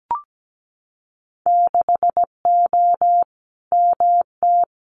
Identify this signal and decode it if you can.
Morse code, heard as a keyed tone, about 17 wpm, 700 Hz. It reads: 6O MT
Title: Winter Field Day Exchanges Encore 17wpm